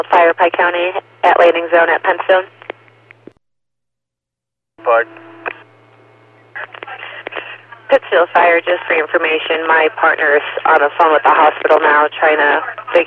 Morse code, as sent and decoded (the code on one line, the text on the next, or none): none